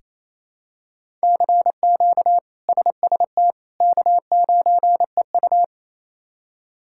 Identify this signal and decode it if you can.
Morse code, heard as a keyed tone, about 28 wpm, 700 Hz. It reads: CQ SST K9EU
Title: K1USN SST QSO Practice - 28wpm